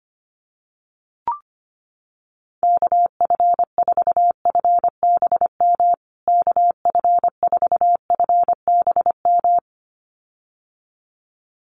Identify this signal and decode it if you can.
Morse code, heard as a keyed tone, about 24 wpm, 700 Hz. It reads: KF4FBM KF4FBM
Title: USA Call Signs Repeated 2x 25wpm